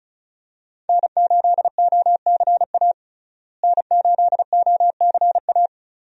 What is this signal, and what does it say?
Morse code, heard as a keyed tone, about 36 wpm, 700 Hz. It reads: N8OCA N8OCA
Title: USA Call Signs - 5 Characters - Repeated 3x - 35wpm